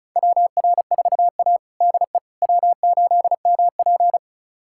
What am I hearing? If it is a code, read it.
Morse code, heard as a keyed tone, about 36 wpm, 700 Hz. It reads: WR4A DE W8MP
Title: USA Callsign DE Callsign - 4 Characters - 35wpm